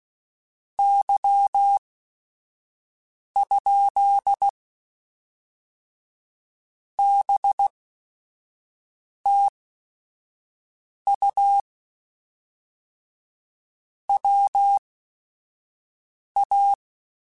Morse code, heard as a keyed tone, about 16 wpm, spaced as for 5 wpm, 800 Hz. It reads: Y? BTU WA
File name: QSOs / QSOs-mp3s / WA7QOX 1@5